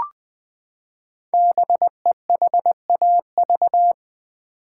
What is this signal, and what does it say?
Morse code, heard as a keyed tone, about 20 wpm, 700 Hz. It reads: BEHAV
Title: Build A Word - 5 Letters At A Time - 10-Letter Words - 20wpm